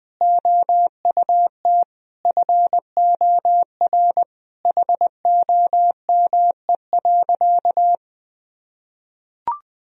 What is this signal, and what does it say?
Morse code, heard as a keyed tone, about 20 wpm, 700 Hz. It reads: OUT FOR HOME.
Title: Sentences from Top 1000 Words - Review Part 1 - 20wpm